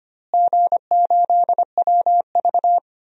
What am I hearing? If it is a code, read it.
Morse code, heard as a keyed tone, about 25 wpm, 700 Hz. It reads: G8WV
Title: DXCC-100 Call Signs - 4 Characters - Repeated 3x - 25wpm